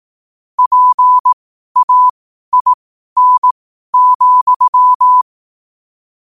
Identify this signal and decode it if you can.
Morse code, heard as a keyed tone, about 18 wpm, spaced as for 12 wpm, 1000 Hz. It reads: PAIN,
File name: qotdfiles / 12wpm